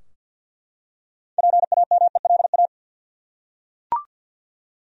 Morse code, heard as a keyed tone, about 50 wpm, 700 Hz. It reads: PAMELA
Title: USA Common Names - 50wpm